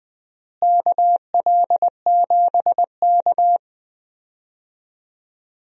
Morse code, heard as a keyed tone, about 21 wpm, 700 Hz. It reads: KL7K
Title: DXCC-100 Call Signs - 4 Characters - Repeated 3x - 20wpm